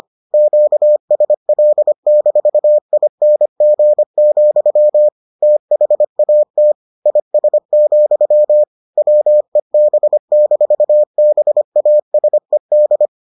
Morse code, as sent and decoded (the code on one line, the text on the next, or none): --.- ... .-.. -....- .. -. --. --..-- / - .... .- - / .. ... --..-- / .-- . -... -....- -... .- ... -..
QSL-ING, THAT IS, WEB-BASED